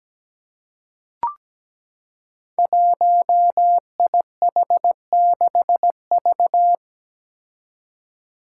Morse code, heard as a keyed tone, about 17 wpm, 700 Hz. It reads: 1IH6V
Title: Five Challenging Letter-Number Combinations 17wpm